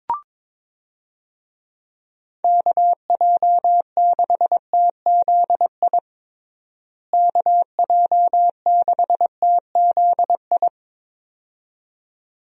Morse code, heard as a keyed tone, about 22 wpm, 700 Hz. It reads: KJ6TZI KJ6TZI
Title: USA Call Signs - 6 Characters - Repeated 2x - 22wpm